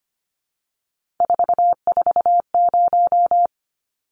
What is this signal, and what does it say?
Morse code, heard as a keyed tone, about 25 wpm, 700 Hz. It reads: 440
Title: Numbers - 3-Digit 25wpm